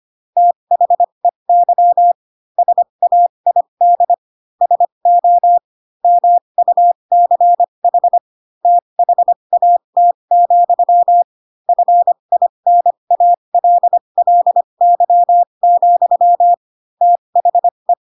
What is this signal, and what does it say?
Morse code, heard as a keyed tone, about 25 wpm, spaced as for 22 wpm, 700 Hz. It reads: THEY SAID SO MUCH THAT, FINALLY, THE
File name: Pinocchio / Pinocchio ch3 18 25 ik1hge